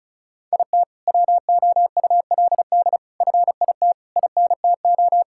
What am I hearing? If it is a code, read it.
Morse code, heard as a keyed tone, about 36 wpm, 700 Hz. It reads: IT WOULD FIT INTO
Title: Sentences from Top 1000 Words - Review Part 5 - 35wpm